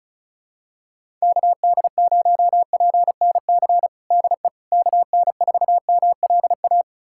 Morse code, heard as a keyed tone, about 34 wpm, 700 Hz. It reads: KD0PNC DE KN4MLA
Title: USA Callsign DE Callsign - 6 Characters - 35wpm